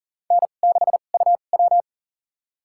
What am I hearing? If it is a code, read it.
Morse code, heard as a keyed tone, about 40 wpm, spaced as for 29 wpm, 700 Hz. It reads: N6UW
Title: USA Call Signs - 4 Characters - 2x Character Spacing - 40wpm